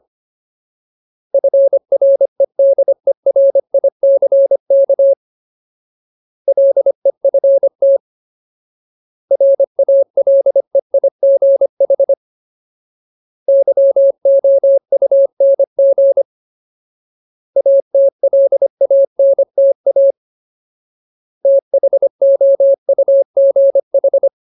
Morse code, heard as a keyed tone, about 25 wpm, 550 Hz. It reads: FREDERICK LEFT RALEIGH YOUNG ATLANTA THOUGH
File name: Ultimate Word List / 25WPM Ultimate